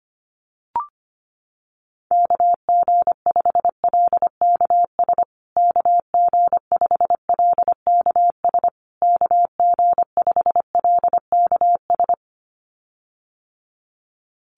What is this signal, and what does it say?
Morse code, heard as a keyed tone, about 25 wpm, 700 Hz. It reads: KG5LKS KG5LKS KG5LKS
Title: USA Call Signs Repeated 3x 25wpm